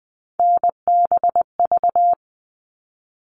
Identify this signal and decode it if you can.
Morse code, heard as a keyed tone, about 20 wpm, 700 Hz. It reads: NBV
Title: All Cut Numbers - 3-Digit 20wpm